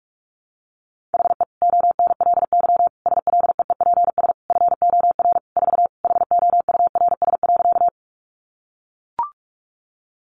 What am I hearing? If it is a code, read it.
Morse code, heard as a keyed tone, about 44 wpm, 700 Hz. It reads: HE ONLY SLEEPS FOR 4 HOURS.
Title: Additional Easy Phrases 45wpm